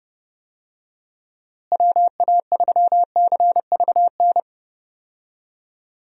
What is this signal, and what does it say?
Morse code, heard as a keyed tone, about 32 wpm, 700 Hz. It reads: WA3CVN
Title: USA Call Signs - 6 Characters - Repeated 3x - 30wpm